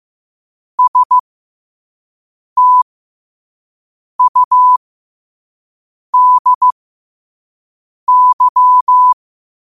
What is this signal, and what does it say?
Morse code, heard as a keyed tone, about 15 wpm, spaced as for 5 wpm, 1000 Hz. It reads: STUDY
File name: qotdfiles / 5wpm